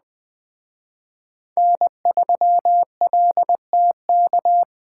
Morse code, heard as a keyed tone, about 20 wpm, 700 Hz. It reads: N3LTK